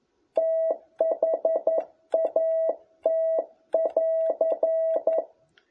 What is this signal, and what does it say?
Morse code, heard as a keyed tone, about 11 wpm, 650 Hz. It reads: THAT<AR>
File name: jh8cbh / 2021cw